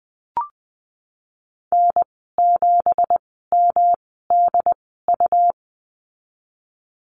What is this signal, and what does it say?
Morse code, heard as a keyed tone, about 19 wpm, spaced as for 14 wpm, 700 Hz. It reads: N7MDU
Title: 173 Lesson 28g - Number 9 - Call Signs 20wpm